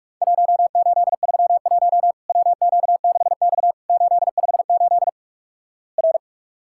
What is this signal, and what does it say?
Morse code, heard as a keyed tone, about 46 wpm, 700 Hz. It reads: 1831 WQ6X 858 R
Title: Worked All Europe DX Contest - QTC Exchange - 45wpm